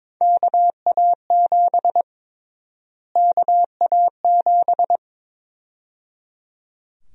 Morse code, heard as a keyed tone, about 22 wpm, 700 Hz. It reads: KA7 KA7
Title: USA Call Sign Prefixes - Repeated 2x - 22wpm